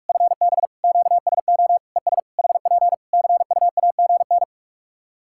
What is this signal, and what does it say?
Morse code, heard as a keyed tone, about 43 wpm, 700 Hz. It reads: FB QSO ES HP CUAGN
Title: K1USN SST QSO Practice 45wpm